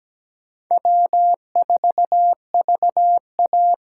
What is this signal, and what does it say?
Morse code, heard as a keyed tone, about 17 wpm, 700 Hz. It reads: W4VA
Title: USA Call Signs - 4 Characters - 2x Character Spacing - 17wpm